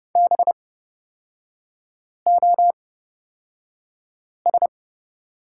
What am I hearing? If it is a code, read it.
Morse code, heard as a keyed tone, about 30 wpm, spaced as for 5 wpm, 700 Hz. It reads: BOS
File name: Lesson 36e - BT - Three Characters 30wpm